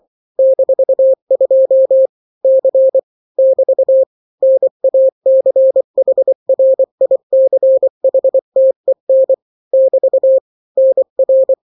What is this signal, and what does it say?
Morse code, heard as a keyed tone, about 24 wpm, 530 Hz. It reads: -2 C <BT> NACHRICHTEN <BT> NR
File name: daily / 23wpm